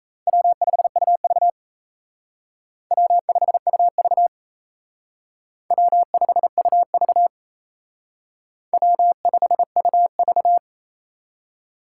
Morse code, heard as a keyed tone, about 35 wpm, 700 Hz. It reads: W5UV W5UV W5UV W5UV
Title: Speed Racing - USA 1x2 Call Signs 28wpm